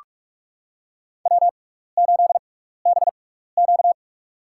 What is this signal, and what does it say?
Morse code, heard as a keyed tone, about 46 wpm, spaced as for 15 wpm, 700 Hz. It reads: W8BQ
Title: USA Call Signs - 4 Characters - 6x Character Spacing - 45wpm